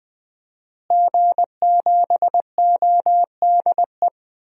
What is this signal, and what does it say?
Morse code, heard as a keyed tone, about 20 wpm, 700 Hz. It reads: G7ODE
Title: DXCC-100 Call Signs - 5 Characters - Repeated 2x - 20wpm